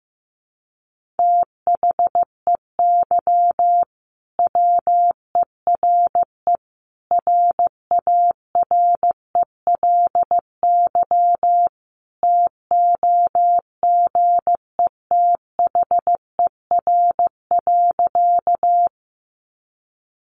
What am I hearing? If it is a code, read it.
Morse code, heard as a keyed tone, about 15 wpm, 700 Hz. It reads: THEY WERE RARELY TOGETHER.
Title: Sentences from Top 1400 Words 15wpm